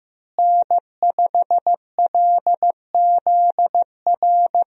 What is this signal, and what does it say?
Morse code, heard as a keyed tone, about 15 wpm, 700 Hz. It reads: N5LZR
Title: USA Call Signs - 5 Characters - Repeated 3x - 15wpm